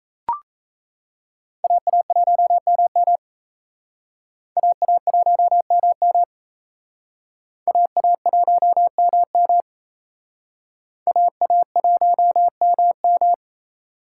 Morse code, heard as a keyed tone, about 35 wpm, 700 Hz. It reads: AA1MM AA1MM AA1MM AA1MM
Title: Speed Racing - USA 2x2 Call Signs 28wpm